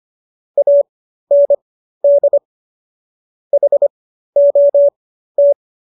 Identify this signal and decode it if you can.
Morse code, heard as a keyed tone, about 25 wpm, spaced as for 13 wpm, 570 Hz. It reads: AND HOT